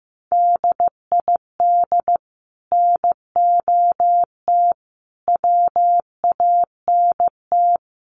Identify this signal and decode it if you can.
Morse code, heard as a keyed tone, about 15 wpm, 700 Hz. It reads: DID NOT WANT
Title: Sets of 4 Words 15wpm